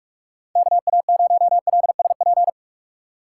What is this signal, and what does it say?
Morse code, heard as a keyed tone, about 42 wpm, 700 Hz. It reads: KA0LSP
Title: USA Call Signs - 6x Character Spacing - 45wpm